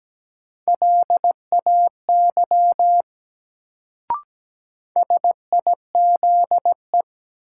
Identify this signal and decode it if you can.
Morse code, heard as a keyed tone, about 17 wpm, 700 Hz. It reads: LAY SIZE
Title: Sentences from Top 400 Words - New Word Drill - 17wpm